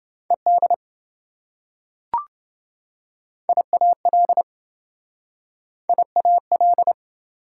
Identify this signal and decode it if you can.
Morse code, heard as a keyed tone, about 26 wpm, 700 Hz. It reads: ED IAL IAL